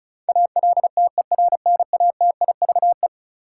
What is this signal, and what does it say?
Morse code, heard as a keyed tone, about 34 wpm, 700 Hz. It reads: ALTERNATIVE
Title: Sentences from Top 1000 Words - Review Part 4 - New Word Drill - 35wpm